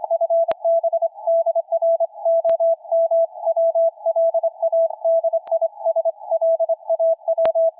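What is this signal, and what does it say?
Morse code, heard as a keyed tone, about 25 wpm, 660 Hz, with radio noise. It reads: FB DR OM WLADISLAW
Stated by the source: Loud portable field-day CW on 40 m from a forest on a dipole, slightly drifting frequency, received through a 300 Hz filter, signal rising from 579 to 9.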